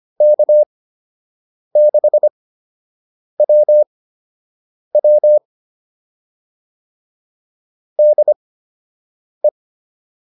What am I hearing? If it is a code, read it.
Morse code, heard as a keyed tone, about 25 wpm, spaced as for 7 wpm, 600 Hz. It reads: K6WW DE